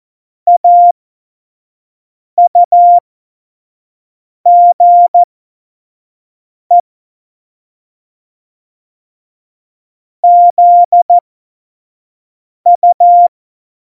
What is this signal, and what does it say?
Morse code, heard as a keyed tone, about 14 wpm, spaced as for 5 wpm, 700 Hz. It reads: AUGE ZU
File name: Morsekurs / Witze / Witz 06. Februar 2018, 25 F70